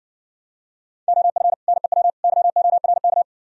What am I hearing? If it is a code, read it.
Morse code, heard as a keyed tone, about 60 wpm, 700 Hz. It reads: OF NEW YORK